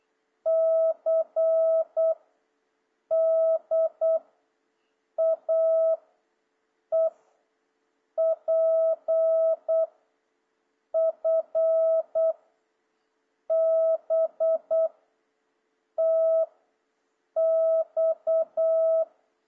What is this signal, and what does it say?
Morse code, heard as a keyed tone, about 8 wpm, 650 Hz. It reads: C D A E P F B T X
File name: jh8cbh / 2026cw